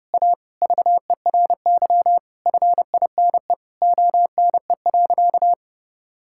Morse code, heard as a keyed tone, about 30 wpm, 700 Hz. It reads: A VERY FINE ONE.
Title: Sentences from Top 1000 Words - Review Part 6 - 30wpm